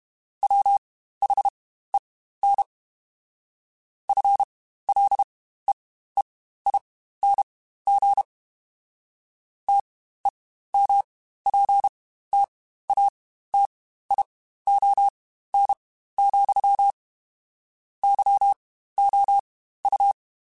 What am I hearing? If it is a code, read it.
Morse code, heard as a keyed tone, about 32 wpm, spaced as for 15 wpm, 800 Hz. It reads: WHEN FLEEING TEMPTATION, YOU
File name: Spaced / Chuckles Spaced-mp3s / Temptation sp @30